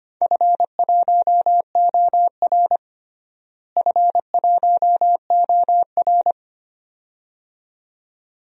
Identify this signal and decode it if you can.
Morse code, heard as a keyed tone, about 25 wpm, 700 Hz. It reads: F1OR F1OR
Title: DXCC-100 Call Signs - 4 Characters - Repeated 2x - 25wpm